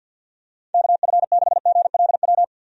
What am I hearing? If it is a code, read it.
Morse code, heard as a keyed tone, about 47 wpm, 700 Hz. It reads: KF6GLW
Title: USA Call Signs - 6 Characters - 50wpm